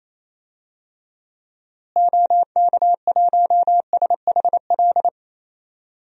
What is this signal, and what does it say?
Morse code, heard as a keyed tone, about 28 wpm, 700 Hz. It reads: OK1SHL
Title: DXCC-100 Call Signs - 6 Characters - Repeated 3x - 28wpm